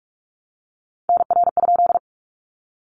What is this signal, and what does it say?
Morse code, heard as a keyed tone, about 44 wpm, 700 Hz. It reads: NR?
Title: Common QSO Phrases 45wpm